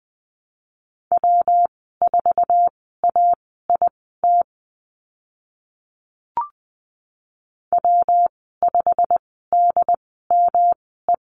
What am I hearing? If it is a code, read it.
Morse code, heard as a keyed tone, about 20 wpm, spaced as for 14 wpm, 700 Hz. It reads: W4AIT W5DME
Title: 122 Lesson 20g - Number 3 - Call Signs 20wpm